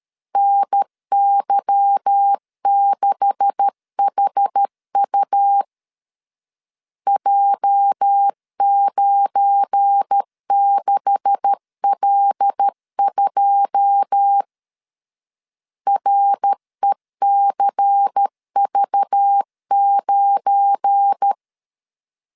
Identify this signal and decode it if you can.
Morse code, heard as a keyed tone, about 13 wpm, 800 Hz. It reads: NY6HU J96L2 RECV9
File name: CW / DET / 1 9 1500C S B 12wpm 3819